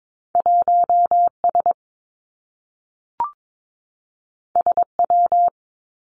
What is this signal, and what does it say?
Morse code, heard as a keyed tone, about 22 wpm, 700 Hz. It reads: 1 S S W
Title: Two Challenging Letter-Number Combinations 22wpm